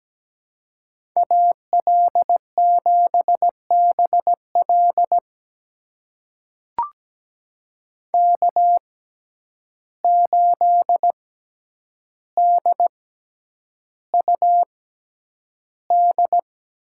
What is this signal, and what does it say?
Morse code, heard as a keyed tone, about 17 wpm, spaced as for 9 wpm, 700 Hz. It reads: AL7BL K8DUD
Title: USA Call Signs - 5 Characters - 6x Character Spacing - 17wpm